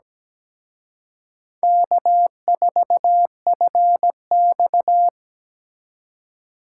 Morse code, heard as a keyed tone, about 20 wpm, 700 Hz. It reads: K4FX
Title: Speed Racing - USA 1x2 Call Signs 17wpm